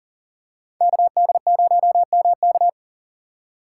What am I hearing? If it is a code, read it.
Morse code, heard as a keyed tone, about 40 wpm, 700 Hz. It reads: KD0MK
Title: USA Call Signs - 5 Characters - 4x Character Spacing - 40wpm